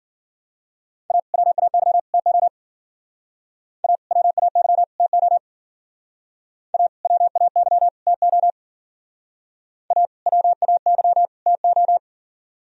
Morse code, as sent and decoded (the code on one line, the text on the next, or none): .- / .-- .- -.-- / - --- / .- / .-- .- -.-- / - --- / .- / .-- .- -.-- / - --- / .- / .-- .- -.-- / - ---
A WAY TO A WAY TO A WAY TO A WAY TO